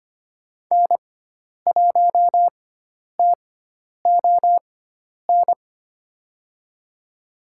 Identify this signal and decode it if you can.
Morse code, heard as a keyed tone, about 25 wpm, spaced as for 10 wpm, 700 Hz. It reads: N1TON
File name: Lesson 16g - Letter c - Call Signs 25wpm